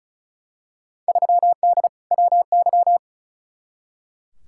Morse code, heard as a keyed tone, about 36 wpm, 700 Hz. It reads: 3D WY
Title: Field Day Exchanges Encore - 35wpm